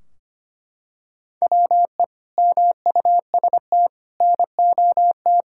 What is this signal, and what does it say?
Morse code, heard as a keyed tone, about 25 wpm, 700 Hz. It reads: WE MUST NOT
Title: Sentences from Top 1600 Words 25wpm